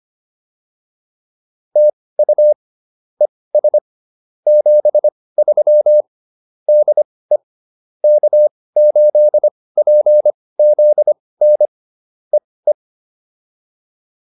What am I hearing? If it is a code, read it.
Morse code, heard as a keyed tone, about 25 wpm, spaced as for 18 wpm, 600 Hz. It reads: TU ES 73 DE K8PZN EE